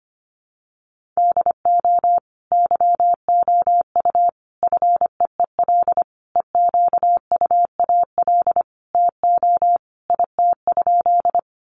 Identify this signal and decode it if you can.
Morse code, heard as a keyed tone, about 25 wpm, 700 Hz. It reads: DO YOU FEEL EQUAL TO IT?
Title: Sentences from Top 1400 Words 25wpm